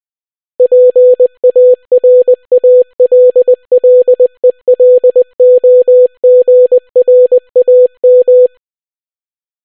Morse code, heard as a keyed tone, about 20 wpm, 500 Hz. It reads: PARALLELOGRAM